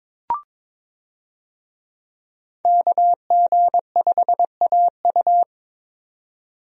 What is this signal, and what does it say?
Morse code, heard as a keyed tone, about 21 wpm, 700 Hz. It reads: KG5AU
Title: USA Call Signs - 5 Characters - 22wpm